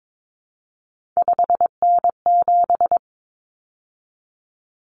Morse code, heard as a keyed tone, about 22 wpm, 700 Hz. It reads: 5N7
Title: Common Cut Numbers - 3-Digit 22wpm